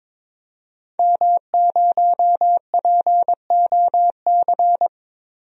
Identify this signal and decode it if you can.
Morse code, heard as a keyed tone, about 22 wpm, 700 Hz. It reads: M0POC
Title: DXCC-100 Call Signs - 5 Characters - 22wpm